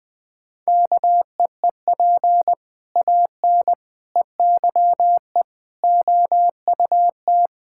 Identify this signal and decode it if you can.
Morse code, heard as a keyed tone, about 19 wpm, 700 Hz. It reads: KEEP AN EYE OUT
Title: Sentences with 6-Word Phrases - 20wpm